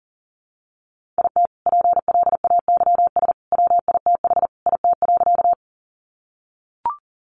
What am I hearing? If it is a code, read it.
Morse code, heard as a keyed tone, about 40 wpm, 700 Hz. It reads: IT PLAYS WITH IT.
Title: Additional Easy Phrases 40wpm